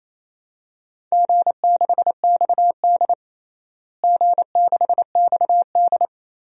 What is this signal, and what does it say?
Morse code, heard as a keyed tone, about 27 wpm, 700 Hz. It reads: G6XD G6XD
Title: DXCC-100 Call Signs - 4 Characters - Repeated 3x - 28wpm